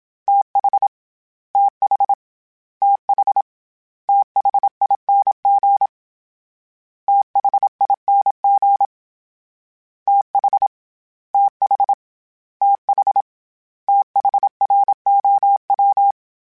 Morse code, heard as a keyed tone, about 27 wpm, 800 Hz. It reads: TH TH TH THING THING TH TH TH THROW
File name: THs / TH / TH Prefixes 3 @25